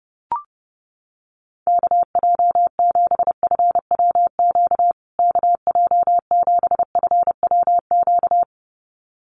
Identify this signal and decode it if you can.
Morse code, heard as a keyed tone, about 30 wpm, 700 Hz. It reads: KJ7FWQ KJ7FWQ
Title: USA Call Signs Repeated 2x 30wpm